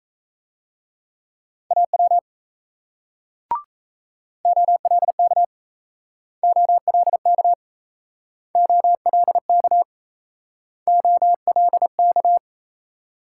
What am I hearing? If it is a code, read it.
Morse code, heard as a keyed tone, about 37 wpm, 700 Hz. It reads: AW OLK OLK OLK OLK